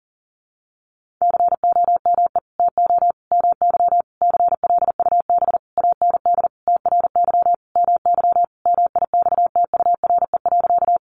Morse code, heard as a keyed tone, about 40 wpm, 700 Hz. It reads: COME TO MY CLUB AND TRY MY MIXTURE.
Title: Sentences from Top 1500 Words 40wpm